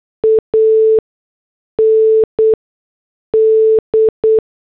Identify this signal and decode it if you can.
Morse code, heard as a keyed tone, about 8 wpm, spaced as for 6 wpm, 440 Hz. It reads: AND